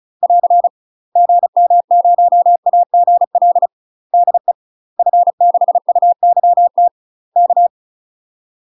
Morse code, heard as a keyed tone, about 36 wpm, 700 Hz. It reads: <AR> GM0AGL DE F6UYT K